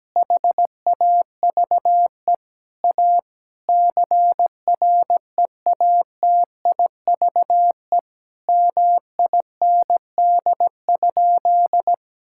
Morse code, heard as a keyed tone, about 17 wpm, 700 Hz. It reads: HAVE A CREATIVE MIND?
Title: Sentences from Top 1000 Words - Review Part 5 - 17wpm